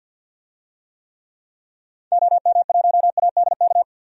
Morse code, heard as a keyed tone, about 47 wpm, 700 Hz. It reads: OM1ADK
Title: DXCC-100 Call Signs - 6 Characters - 50wpm